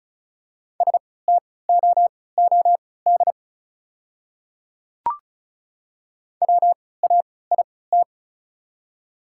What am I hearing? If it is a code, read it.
Morse code, heard as a keyed tone, about 36 wpm, spaced as for 20 wpm, 700 Hz. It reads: STOOD WAIT
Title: 3x Character Spacing - Top 500 Words - 35wpm